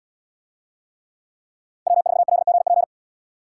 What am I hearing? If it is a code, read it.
Morse code, heard as a keyed tone, about 77 wpm, 700 Hz. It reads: W4LGP